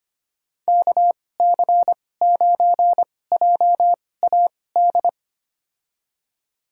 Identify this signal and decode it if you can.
Morse code, heard as a keyed tone, about 25 wpm, spaced as for 18 wpm, 700 Hz. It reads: KC9JAD
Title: USA Call Signs - 6 Characters - 2x Character Spacing - 25wpm